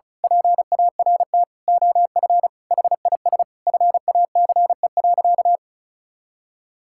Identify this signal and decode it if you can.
Morse code, heard as a keyed tone, about 34 wpm, 700 Hz. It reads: PART OF HIS FACE.
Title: Sentences from Top 1000 Words - Review Part 5 - 35wpm